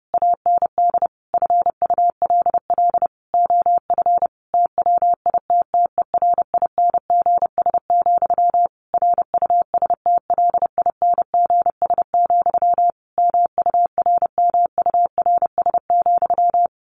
Morse code, heard as a keyed tone, about 30 wpm, 700 Hz. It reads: AND FULL OF TWITTERINGS, RUSTLINGS, MURMURS,
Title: A Sound Of Thunder 30wpm